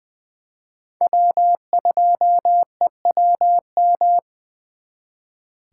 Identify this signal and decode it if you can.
Morse code, heard as a keyed tone, about 24 wpm, 700 Hz. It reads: W2EWM
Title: Speed Racing - USA 1x3 Call Signs 20wpm